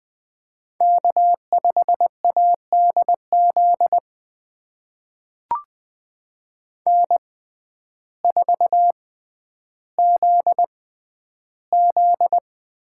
Morse code, heard as a keyed tone, about 20 wpm, spaced as for 11 wpm, 700 Hz. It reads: K5ADZ N4ZZ
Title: USA Call Signs - 6x Character Spacing - 20wpm